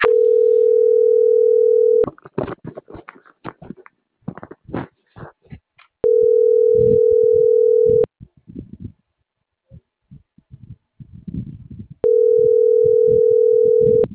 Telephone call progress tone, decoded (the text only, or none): ringback tone